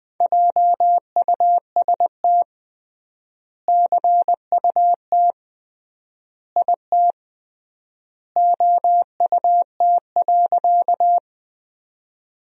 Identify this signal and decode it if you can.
Morse code, heard as a keyed tone, about 21 wpm, 700 Hz. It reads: JUST CUT IT OUT.
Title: 3x Word-Spacing - Sentences from Top 300 Words - 20wpm